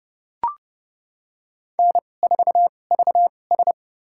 Morse code, heard as a keyed tone, about 32 wpm, spaced as for 22 wpm, 700 Hz. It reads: N4VS
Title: USA Call Signs - 4 Characters - 2x Character Spacing - 30wpm